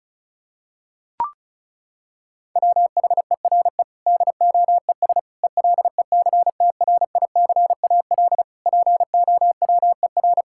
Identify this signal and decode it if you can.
Morse code, heard as a keyed tone, about 36 wpm, 700 Hz. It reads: WHERE DOES ELECTRICAL POWER
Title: Sentences from Top 1000 Words - Review Part 6 - 35wpm